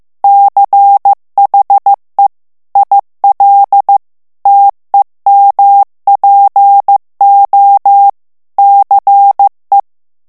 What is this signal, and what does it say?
Morse code, heard as a keyed tone, about 15 wpm, 800 Hz. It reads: CHE IL TEMPO CE
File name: ITALIEN texte clair 1 vitesse 14 mots minute